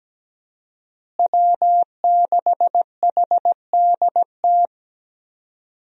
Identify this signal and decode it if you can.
Morse code, heard as a keyed tone, about 17 wpm, 700 Hz. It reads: W6HDT
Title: USA Call Signs - 5 Characters - Repeated 3x - 17wpm